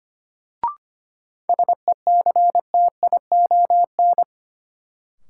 Morse code, heard as a keyed tone, about 25 wpm, 700 Hz. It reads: SECTION
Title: Sentences from Top 1000 Words - Review Part 5 - New Word Drill - 25wpm